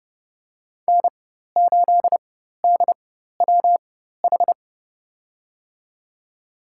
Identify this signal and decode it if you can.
Morse code, heard as a keyed tone, about 31 wpm, spaced as for 14 wpm, 700 Hz. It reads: N8DWH
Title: USA Call Signs - 5 Characters - 4x Character Spacing - 30wpm